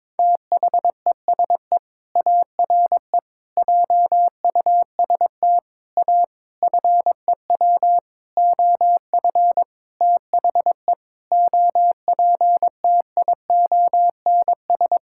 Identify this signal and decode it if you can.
Morse code, heard as a keyed tone, about 22 wpm, 700 Hz. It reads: THESE ARE JUST A FEW OF THE OPTIONS
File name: Sentences with 7-Word Phrases - 22wpm